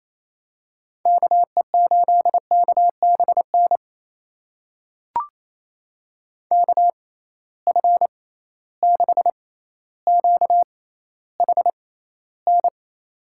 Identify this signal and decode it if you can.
Morse code, heard as a keyed tone, about 28 wpm, spaced as for 14 wpm, 700 Hz. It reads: KE8KBN KF6QHN
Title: USA Call Signs - 6 Characters - 6x Character Spacing - 28wpm